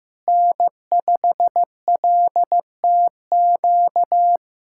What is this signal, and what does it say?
Morse code, heard as a keyed tone, about 16 wpm, 700 Hz. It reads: N5LTQ
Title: USA Callsign DE Callsign - 5 Characters - 15wpm